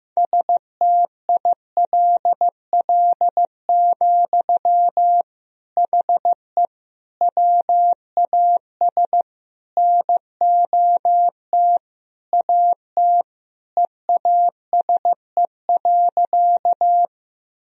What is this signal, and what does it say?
Morse code, heard as a keyed tone, about 15 wpm, 700 Hz. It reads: STILL, HE WAS NOT AT EASE.
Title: Sentences from Top 1000 Words - Review Part 1 - 15wpm